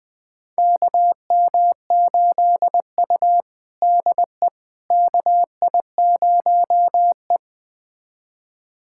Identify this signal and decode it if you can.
Morse code, heard as a keyed tone, about 20 wpm, 700 Hz. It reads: KM8U DE KI0E
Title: USA Callsign DE Callsign - 4 Characters - 20wpm